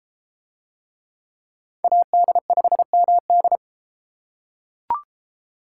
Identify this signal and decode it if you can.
Morse code, heard as a keyed tone, about 30 wpm, 700 Hz. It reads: AD5MD